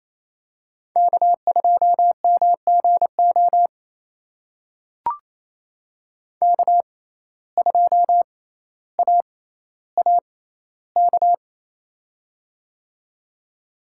Morse code, heard as a keyed tone, about 28 wpm, spaced as for 14 wpm, 700 Hz. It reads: K2MGO K2AAK
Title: USA Call Signs - 5 Characters - 6x Character Spacing - 28wpm